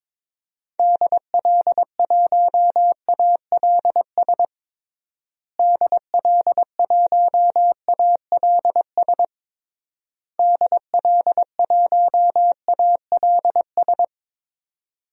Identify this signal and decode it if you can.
Morse code, heard as a keyed tone, about 22 wpm, 700 Hz. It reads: DL1ALS DL1ALS DL1ALS
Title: DXCC-100 Call Signs - 6 Characters - Repeated 3x - 22wpm